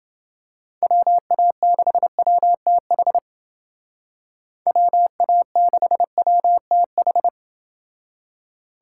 Morse code, heard as a keyed tone, about 30 wpm, 700 Hz. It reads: WA6WTH WA6WTH